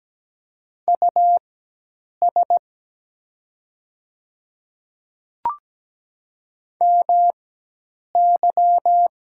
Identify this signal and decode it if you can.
Morse code, heard as a keyed tone, about 17 wpm, spaced as for 8 wpm, 700 Hz. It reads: US MY
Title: 4x Character Spacing - Top 200 Words - 17wpm